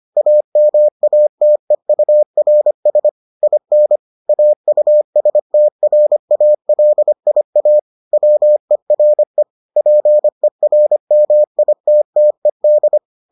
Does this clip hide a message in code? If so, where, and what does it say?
Yes: Morse code from 0.2 to 13.0 s, heard as a keyed tone, about 25 wpm, 600 Hz. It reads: AMATEURS IN AUSTRALIA WERE PERMITTED